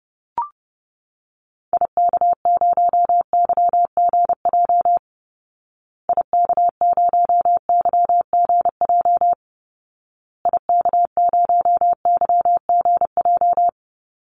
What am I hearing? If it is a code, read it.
Morse code, heard as a keyed tone, about 30 wpm, 700 Hz. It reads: IK0YGJ IK0YGJ IK0YGJ
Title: Call Signs Repeated 2x - 30wpm